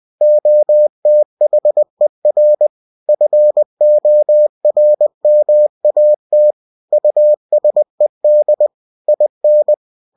Morse code, heard as a keyed tone, about 20 wpm, 600 Hz. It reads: OTHER FORMAT USED IN
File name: cw-books / winnie-the-pooh / winnie-20wpm-page-0116